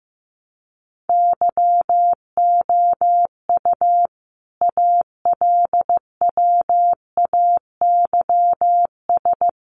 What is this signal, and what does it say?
Morse code, heard as a keyed tone, about 15 wpm, 700 Hz. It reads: YOU ALWAYS
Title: Sentences from Top 700 Words 15wpm